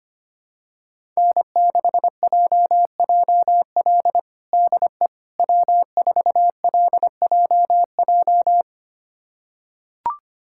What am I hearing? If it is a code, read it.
Morse code, heard as a keyed tone, about 25 wpm, 700 Hz. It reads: N6JJL DE W4LJJ
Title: USA Callsign DE Callsign - 5 Characters - 25wpm